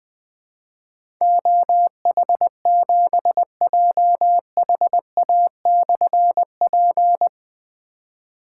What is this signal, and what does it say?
Morse code, heard as a keyed tone, about 20 wpm, 700 Hz. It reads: OH7JHA/P
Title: Difficult Call Signs 20wpm